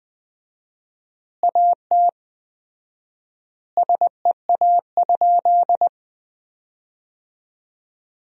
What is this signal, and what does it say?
Morse code, heard as a keyed tone, about 20 wpm, 700 Hz. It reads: AT SEA?
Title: 4x Word-Spacing - Sentences from Top 300 Words - 20wpm